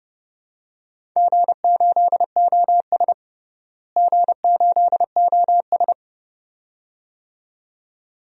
Morse code, heard as a keyed tone, about 30 wpm, 700 Hz. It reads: G8OS G8OS
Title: DXCC-100 Call Signs - 4 Characters - Repeated 2x - 30wpm